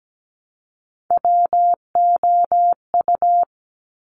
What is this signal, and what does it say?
Morse code, heard as a keyed tone, about 17 wpm, 700 Hz. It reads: WOU
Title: Top Three Letter Combinations in Words 17wpm